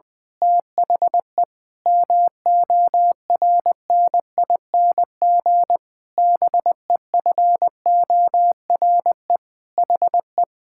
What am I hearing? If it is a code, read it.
Morse code, heard as a keyed tone, about 20 wpm, 700 Hz. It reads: THE MORNING BEFORE HE
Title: Pangram Challenge v2 - 20wpm